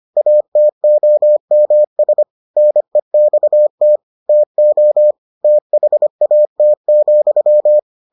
Morse code, heard as a keyed tone, about 25 wpm, 600 Hz. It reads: ATOMS NEXT TO THAT,